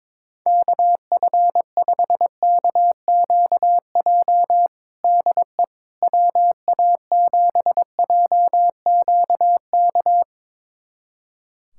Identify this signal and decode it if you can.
Morse code, heard as a keyed tone, about 22 wpm, 700 Hz. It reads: KF5KQJ DE WA7JQK
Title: USA Callsign DE Callsign 22wpm